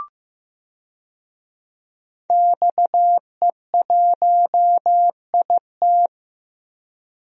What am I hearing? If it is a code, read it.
Morse code, heard as a keyed tone, about 15 wpm, 700 Hz. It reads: XE1IT
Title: DXCC-100 Call Signs - 5 Characters - Repeated 2x - 15wpm